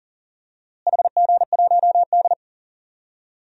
Morse code, heard as a keyed tone, about 40 wpm, 700 Hz. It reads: HG1D
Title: DXCC-100 Call Signs - 4 Characters - Repeated 2x - 40wpm